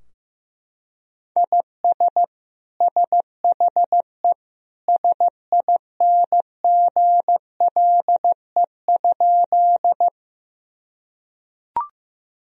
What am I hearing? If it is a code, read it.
Morse code, heard as a keyed tone, about 15 wpm, 700 Hz. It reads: IS SHE SINGLE?
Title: Sentences from Top 1000 Words - Review Part 4 - 15wpm